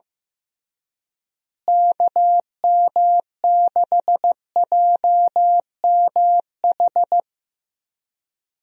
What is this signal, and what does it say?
Morse code, heard as a keyed tone, about 15 wpm, 700 Hz. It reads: KM6JMH
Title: USA Call Signs - 6 Characters - Repeated 3x - 15wpm